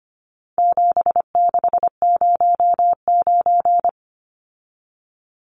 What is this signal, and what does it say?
Morse code, heard as a keyed tone, about 25 wpm, 700 Hz. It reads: 7609
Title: Numbers - 4-Digit 25wpm